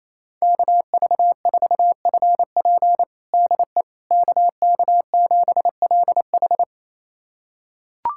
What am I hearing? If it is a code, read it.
Morse code, heard as a keyed tone, about 27 wpm, 700 Hz. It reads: KV4FP DE KK7LH
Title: USA Callsign DE Callsign - 5 Characters - 28wpm